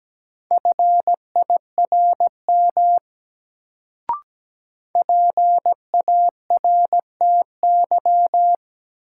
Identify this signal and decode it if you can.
Morse code, heard as a keyed tone, about 17 wpm, 700 Hz. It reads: FIRM PARTY
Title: Sentences from Top 1400 Words - New Word Drill Alt - 17wpm